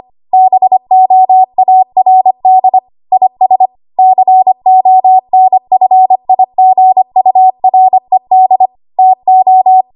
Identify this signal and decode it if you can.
Morse code, heard as a keyed tone, about 25 wpm, 750 Hz. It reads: BOARD IS CONFIGURED TO
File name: Archive / 25 WPM / 25WPM